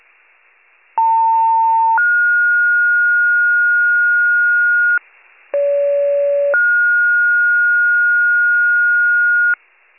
Two-tone page, heard as a one-second tone, then a three-second tone, about 900 then 1430 Hz, about 570 then 1430 Hz. It is heard from 1.0 to 9.5 s.
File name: wcalert